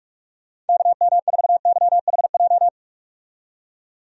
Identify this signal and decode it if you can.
Morse code, heard as a keyed tone, about 46 wpm, 700 Hz. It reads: KM4YHJ
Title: USA Call Signs - 6 Characters - 45wpm